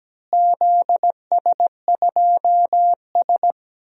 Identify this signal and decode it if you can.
Morse code, heard as a keyed tone, about 17 wpm, 700 Hz. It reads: ZS2S
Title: DXCC-100 Call Signs - 4 Characters - Repeated 2x - 17wpm